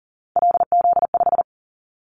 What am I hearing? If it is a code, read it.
Morse code, heard as a keyed tone, about 37 wpm, 700 Hz. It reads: LZ5
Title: Callsign Prefixes for Top 50 DXCC - 40wpm